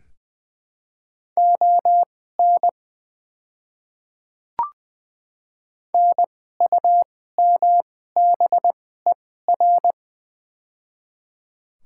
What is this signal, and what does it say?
Morse code, heard as a keyed tone, about 20 wpm, spaced as for 14 wpm, 700 Hz. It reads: ON NUMBER
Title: 2x Character Spacing - Top 100 Words - 20wpm